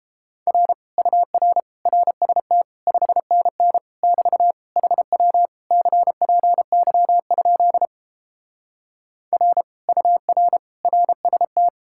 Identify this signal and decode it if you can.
Morse code, heard as a keyed tone, about 29 wpm, 700 Hz. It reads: R UR RST 5NN <BT> HW CPY? R UR RST